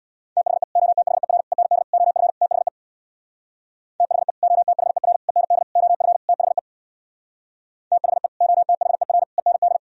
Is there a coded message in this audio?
Yes: Morse code from 0.4 to 9.8 s, heard as a keyed tone, about 64 wpm, 700 Hz. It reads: THE OTHER END OF THE THE OTHER END OF THE THE OTHER END